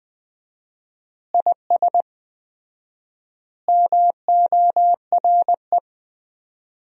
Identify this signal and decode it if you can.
Morse code, heard as a keyed tone, about 20 wpm, 700 Hz. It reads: IS MORE
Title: 4x Word-Spacing - Sentences from Top 100 Words - 20wpm